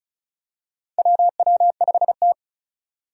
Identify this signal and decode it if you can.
Morse code, heard as a keyed tone, about 34 wpm, 700 Hz. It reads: WW5T
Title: USA Call Signs - 4 Characters - 35wpm